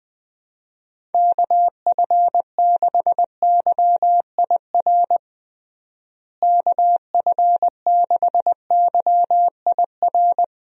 Morse code, heard as a keyed tone, about 21 wpm, 700 Hz. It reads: KF6YIR KF6YIR
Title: USA Call Signs - 6 Characters - Repeated 2x - 20wpm